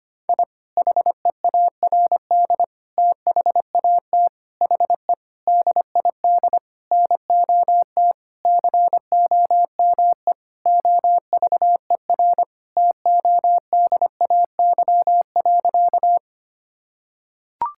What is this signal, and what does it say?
Morse code, heard as a keyed tone, about 25 wpm, 700 Hz. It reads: I HEARD THAT HE DID NOT COME OVER TODAY.
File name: practice / 25wpm / Sentences with 4-Word Phrases - 25wpm